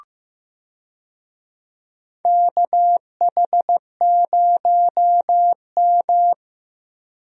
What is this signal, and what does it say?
Morse code, heard as a keyed tone, about 15 wpm, 700 Hz. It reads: KH0M
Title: USA Call Signs - 4 Characters - Repeated 2x - 15wpm